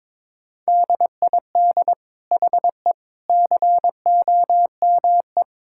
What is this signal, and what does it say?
Morse code, heard as a keyed tone, about 21 wpm, 700 Hz. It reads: DID HE COME
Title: Sentences from Top 1000 Words - Review Part 5 - 22wpm